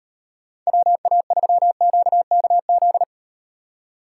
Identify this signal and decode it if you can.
Morse code, heard as a keyed tone, about 38 wpm, 700 Hz. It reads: WA3QKZ